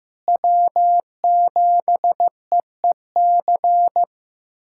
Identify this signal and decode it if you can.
Morse code, heard as a keyed tone, about 15 wpm, 700 Hz. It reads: W7EEC
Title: USA Callsign DE Callsign - 5 Characters - 15wpm